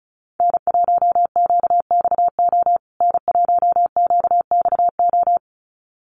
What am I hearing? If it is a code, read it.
Morse code, heard as a keyed tone, about 34 wpm, 700 Hz. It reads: N1QXO N1QXO
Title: USA Call Signs Repeated 2x 35wpm